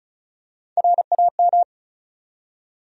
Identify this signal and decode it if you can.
Morse code, heard as a keyed tone, about 34 wpm, 700 Hz. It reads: RAM
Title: USA Call Sign Postfixes - 3 Letters 35wpm